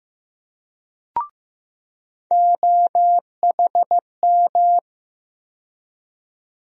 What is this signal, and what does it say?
Morse code, heard as a keyed tone, about 15 wpm, 700 Hz. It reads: OHM
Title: USA Call Sign Postfixes - 3 Letters 15wpm